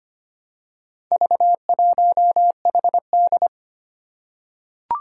Morse code, heard as a keyed tone, about 25 wpm, 700 Hz. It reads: V1HD
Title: Four Challenging Letter-Number Combinations 25wpm